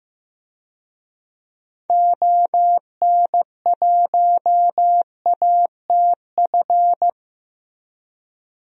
Morse code, heard as a keyed tone, about 15 wpm, 700 Hz. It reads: ON1ATF
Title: DXCC-100 Call Signs - 6 Characters - Repeated 3x - 15wpm